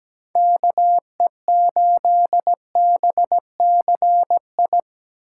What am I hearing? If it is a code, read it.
Morse code, heard as a keyed tone, about 17 wpm, 700 Hz. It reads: KE8BCI
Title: USA Call Signs - 6 Characters - 8x Character Spacing - 17wpm